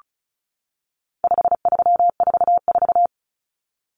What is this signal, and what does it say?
Morse code, heard as a keyed tone, about 34 wpm, 700 Hz. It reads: 5344
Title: Numbers - 4-Digit 35wpm